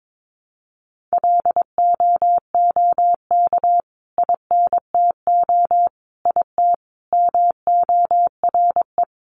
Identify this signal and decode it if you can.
Morse code, heard as a keyed tone, about 22 wpm, 700 Hz. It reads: LOOK INTO IT MORE
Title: Sentences from Top 1300 Words 22wpm